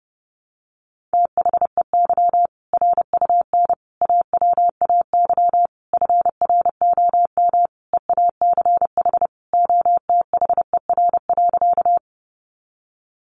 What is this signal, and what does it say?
Morse code, heard as a keyed tone, about 28 wpm, 700 Hz. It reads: THEY RUN AWAY FROM EACH OTHER.
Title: Easy Phrases 30wpm